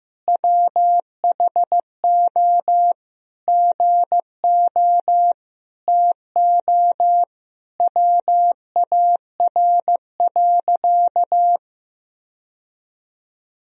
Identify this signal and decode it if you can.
Morse code, heard as a keyed tone, about 15 wpm, 700 Hz. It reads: WHO GO TO WAR.
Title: Sentences from Top 1000 Words - Review Part 2 - 15wpm